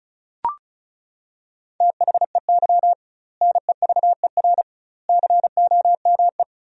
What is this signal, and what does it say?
Morse code, heard as a keyed tone, about 34 wpm, 700 Hz. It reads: THEY NEVER COME
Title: 2x Word-Spacing - Sentences from Top 300 Words - 35wpm